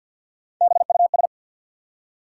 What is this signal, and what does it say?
Morse code, heard as a keyed tone, about 53 wpm, 700 Hz. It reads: BUS
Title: Three Letter Words - 50wpm